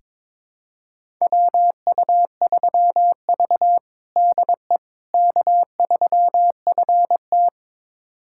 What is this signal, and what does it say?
Morse code, heard as a keyed tone, about 22 wpm, 700 Hz. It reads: WU3V DE K3FT
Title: USA Callsign DE Callsign - 4 Characters - 22wpm